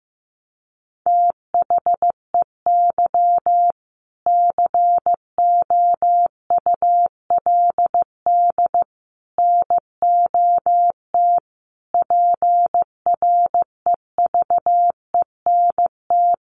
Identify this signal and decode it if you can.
Morse code, heard as a keyed tone, about 15 wpm, 700 Hz. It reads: THEY COULD NOT PREVENT
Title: Sentences from Top 1100 Words 15wpm